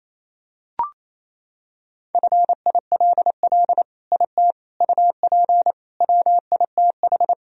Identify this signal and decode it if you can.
Morse code, heard as a keyed tone, about 27 wpm, 700 Hz. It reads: FILL IT UP WITH
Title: Sentences from Top 1000 Words - Review Part 3 - 28wpm